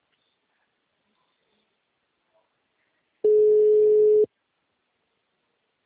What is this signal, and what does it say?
Ringback tone (Europe, 425 Hz).
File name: out- -2002- - - .64